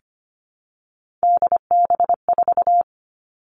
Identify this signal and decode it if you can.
Morse code, heard as a keyed tone, about 25 wpm, 700 Hz. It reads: DB4
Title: Three Challenging Letter-Number Combinations 25wpm